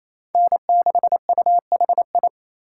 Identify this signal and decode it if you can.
Morse code, heard as a keyed tone, about 28 wpm, 700 Hz. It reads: N6UHI
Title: USA Call Signs - 5 Characters - 4x Character Spacing - 28wpm